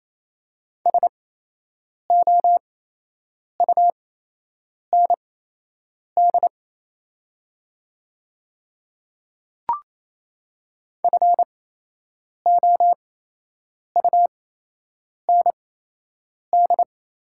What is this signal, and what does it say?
Morse code, heard as a keyed tone, about 28 wpm, spaced as for 8 wpm, 700 Hz. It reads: SOUND FOUND
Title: 8x Character Spacing - Top 500 Words - 28wpm